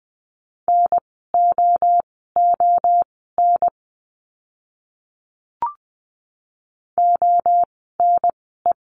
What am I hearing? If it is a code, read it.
Morse code, heard as a keyed tone, about 20 wpm, spaced as for 14 wpm, 700 Hz. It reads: NOON ONE
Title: 030 Lesson 6f - Letter i - Words 20wpm